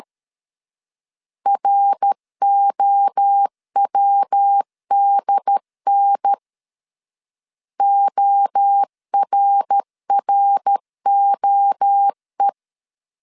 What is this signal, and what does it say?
Morse code, heard as a keyed tone, about 13 wpm, 800 Hz. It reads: ROWDN ORROE
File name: CW / DET / 1 3 1500C S B 12wpm 2828